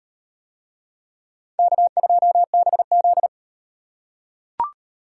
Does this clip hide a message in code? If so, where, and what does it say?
Yes: Morse code from 1.6 to 3.3 s, heard as a keyed tone, about 37 wpm, 700 Hz. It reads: K2BZ